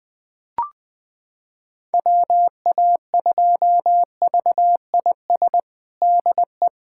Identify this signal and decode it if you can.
Morse code, heard as a keyed tone, about 20 wpm, 700 Hz. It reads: WA2VIS DE
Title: USA Callsign DE Callsign - 6 Characters - 20wpm